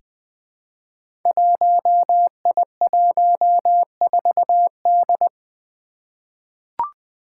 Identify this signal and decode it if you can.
Morse code, heard as a keyed tone, about 20 wpm, 700 Hz. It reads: 1I14D
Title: Five Challenging Letter-Number Combinations 20wpm